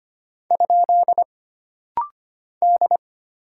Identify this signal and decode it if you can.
Morse code, heard as a keyed tone, about 25 wpm, 700 Hz. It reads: ? D
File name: Single Character (Letter:Number:Punctuation:Prosign) - Rapid-Fire - 25wpm